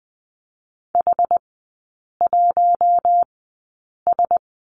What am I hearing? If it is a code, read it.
Morse code, heard as a keyed tone, about 20 wpm, spaced as for 8 wpm, 690 Hz. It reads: H1S
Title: 081 Lesson 14e - Number 2 - Three Characters 20wpm